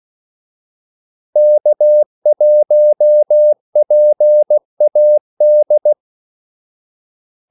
Morse code, heard as a keyed tone, about 16 wpm, 600 Hz. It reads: K1PAD
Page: Kaldesignaler 16 wpm